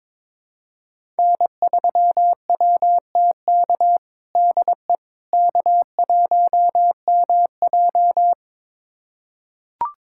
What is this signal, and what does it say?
Morse code, heard as a keyed tone, about 22 wpm, 700 Hz. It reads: N3WTK DE K1MJ
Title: USA Callsign DE Callsign 22wpm